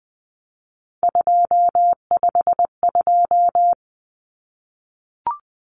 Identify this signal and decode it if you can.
Morse code, heard as a keyed tone, about 20 wpm, 700 Hz. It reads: 252
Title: Common Cut Numbers - 3-Digit 20wpm